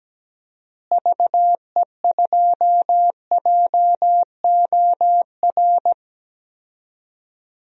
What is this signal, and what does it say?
Morse code, heard as a keyed tone, about 17 wpm, 700 Hz. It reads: VE2JOR
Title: DXCC-100 Call Signs - 6 Characters - Repeated 2x - 17wpm